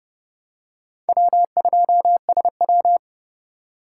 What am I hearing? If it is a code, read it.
Morse code, heard as a keyed tone, about 30 wpm, 700 Hz. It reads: W2SW
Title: USA Call Signs - 2x Character Spacing - 30wpm